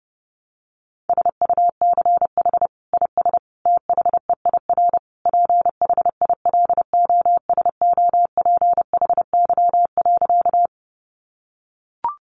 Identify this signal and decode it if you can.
Morse code, heard as a keyed tone, about 28 wpm, 700 Hz. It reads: SUCH IS THEIR PHILOSOPHY.
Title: Sentences from Top 900 Words 30wpm